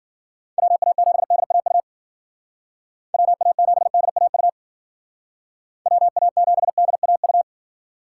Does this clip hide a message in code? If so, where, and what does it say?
Yes: Morse code from 0.6 to 7.4 s, heard as a keyed tone, about 52 wpm, 700 Hz. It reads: WA7DAU WA7DAU WA7DAU